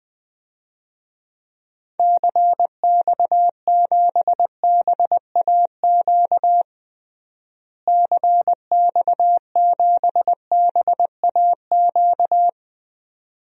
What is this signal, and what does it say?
Morse code, heard as a keyed tone, about 20 wpm, 700 Hz. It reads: CX7BAQ CX7BAQ
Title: DXCC-100 Call Signs - 6 Characters - Repeated 2x - 20wpm